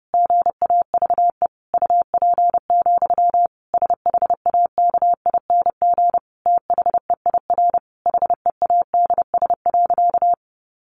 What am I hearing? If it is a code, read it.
Morse code, heard as a keyed tone, about 30 wpm, 700 Hz. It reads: GAVE UP, SHAKING THEIR HEADS.
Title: A Sound Of Thunder 30wpm